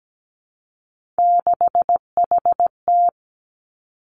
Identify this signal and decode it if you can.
Morse code, heard as a keyed tone, about 17 wpm, 700 Hz. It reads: 6HT
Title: Three Challenging Letter-Number Combinations 17wpm